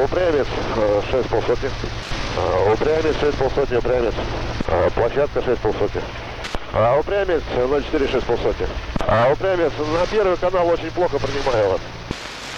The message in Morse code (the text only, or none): none